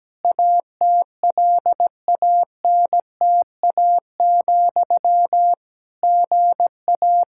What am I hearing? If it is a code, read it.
Morse code, heard as a keyed tone, about 17 wpm, 700 Hz. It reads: ATLANTA, GA
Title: Practice QSO - Exchange 1 - Beginning - 17wpm